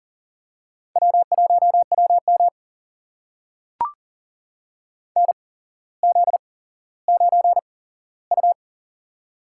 Morse code, heard as a keyed tone, about 40 wpm, spaced as for 17 wpm, 700 Hz. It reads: W1WM NZ9U
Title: USA Call Signs - 4 Characters - 8x Character Spacing - 40wpm